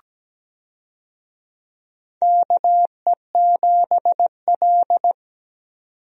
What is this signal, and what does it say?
Morse code, heard as a keyed tone, about 17 wpm, 700 Hz. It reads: KE7L
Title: USA Call Signs - 4 Characters - 17wpm